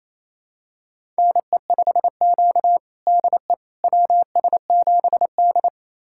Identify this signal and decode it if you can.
Morse code, heard as a keyed tone, about 29 wpm, 690 Hz. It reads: NE5Q DE WS7D
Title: USA Callsign DE Callsign - 4 Characters - 28wpm